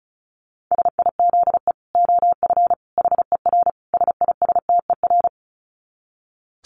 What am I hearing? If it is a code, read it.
Morse code, heard as a keyed tone, about 34 wpm, 700 Hz. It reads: SIZE OF HER SISTER
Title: Dit Challenge 35wpm